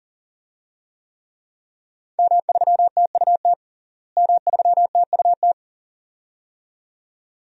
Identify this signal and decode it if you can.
Morse code, heard as a keyed tone, about 37 wpm, 700 Hz. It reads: M3TUT M3TUT
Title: DXCC-100 Call Signs - 5 Characters - Repeated 2x - 40wpm